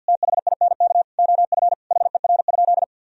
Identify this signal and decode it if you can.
Morse code, heard as a keyed tone, about 51 wpm, 700 Hz. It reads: THINK OF HER?
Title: Sentences from Top 1000 Words - Review Part 4 - 50wpm